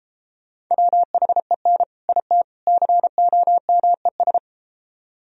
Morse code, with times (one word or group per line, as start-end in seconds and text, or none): 0.7-1.8 WHEN
2.1-2.4 IT
2.7-4.4 COMES